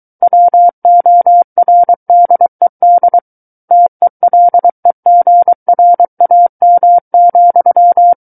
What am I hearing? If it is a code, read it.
Morse code, heard as a keyed tone, about 23 wpm, 700 Hz. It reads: WORDED TELEGRAM,
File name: Book One / 14-Chapter 14 - 23 WPM